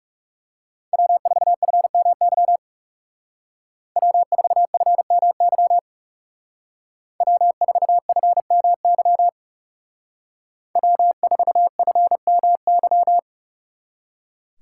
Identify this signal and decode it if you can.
Morse code, heard as a keyed tone, about 34 wpm, 700 Hz. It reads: W4FMY W4FMY W4FMY W4FMY
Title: Speed Racing - USA 1x3 Call Signs 30wpm